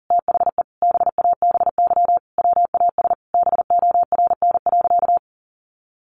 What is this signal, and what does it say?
Morse code, heard as a keyed tone, about 40 wpm, 700 Hz. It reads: THE BABY WAS BORN.
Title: Sentences from Top 1300 Words 40wpm